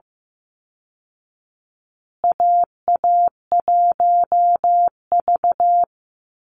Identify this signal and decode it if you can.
Morse code, heard as a keyed tone, about 15 wpm, 700 Hz. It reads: AA1V
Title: Call Signs Repeated 3x - 15wpm